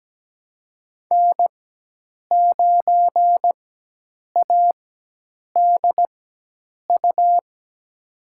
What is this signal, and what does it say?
Morse code, heard as a keyed tone, about 17 wpm, spaced as for 8 wpm, 700 Hz. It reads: N9ADU
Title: USA Call Signs - 5 Characters - 4x Character Spacing - 17wpm